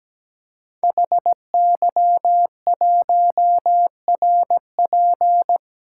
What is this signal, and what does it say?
Morse code, heard as a keyed tone, about 17 wpm, 700 Hz. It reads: HY1RP
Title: Code Groups - 5 Letter:Numbers - 17wpm